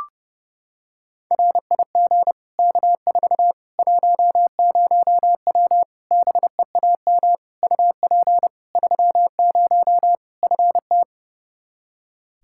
Morse code, heard as a keyed tone, about 32 wpm, 700 Hz. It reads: RIG K4 10W BEAM UP 30 FT
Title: K1USN SST QSO Practice 30wpm